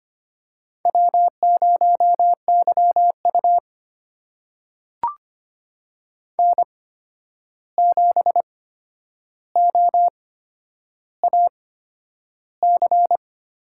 Morse code, heard as a keyed tone, about 25 wpm, spaced as for 10 wpm, 700 Hz. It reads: W0YU N7OAC
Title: USA Call Signs - 8x Character Spacing - 25wpm